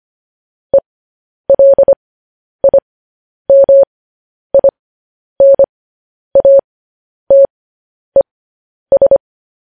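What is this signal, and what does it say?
Morse code, heard as a keyed tone, about 25 wpm, spaced as for 10 wpm, 550 Hz. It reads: ELIMINATES
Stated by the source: Morse news headlines sent at 10 wpm.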